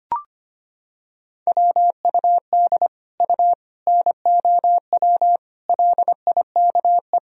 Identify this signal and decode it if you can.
Morse code, heard as a keyed tone, about 25 wpm, 700 Hz. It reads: WUD U NOW LIKE
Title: Sentences from Top 200 Words and Abbreviations 25wpm